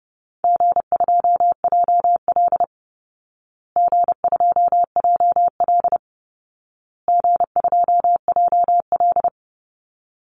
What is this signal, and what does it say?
Morse code, heard as a keyed tone, about 30 wpm, 700 Hz. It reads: G2JL G2JL G2JL
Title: Easy Call Signs Repeated 3x - 30wpm